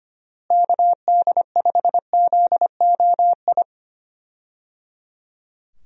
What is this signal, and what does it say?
Morse code, heard as a keyed tone, about 25 wpm, 700 Hz. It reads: KD5ZOI
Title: USA Call Signs - 6 Characters - Repeated 3x - 25wpm